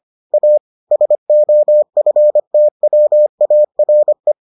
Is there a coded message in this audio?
Yes: Morse code from 0.3 to 4.3 s, heard as a keyed tone, about 25 wpm, 600 Hz. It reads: A SOFTWARE